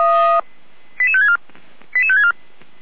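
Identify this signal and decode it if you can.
Five-tone selective call (ZVEI), heard as quick runs of five tones, tones of about 70 ms. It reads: 88564 88564
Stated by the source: ZVEI five-tone call received by SDR.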